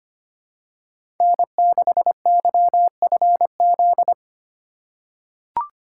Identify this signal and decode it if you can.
Morse code, heard as a keyed tone, about 25 wpm, 700 Hz. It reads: N6YFZ
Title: USA Call Signs - 3x Character Spacing - 25wpm